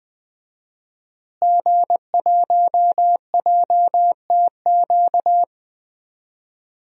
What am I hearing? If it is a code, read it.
Morse code, heard as a keyed tone, about 20 wpm, 700 Hz. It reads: G1JTQ
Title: DXCC-100 Call Signs - 5 Characters - Repeated 3x - 20wpm